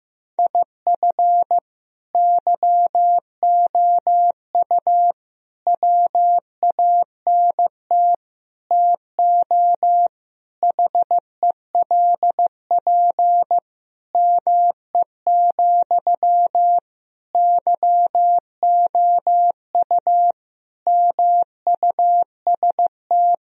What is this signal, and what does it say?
Morse code, heard as a keyed tone, about 15 wpm, 700 Hz. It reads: IF YOU WANT TO HELP ME, YOU MUST
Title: Sentences from Top 1000 Words - Review Part 1 - 15wpm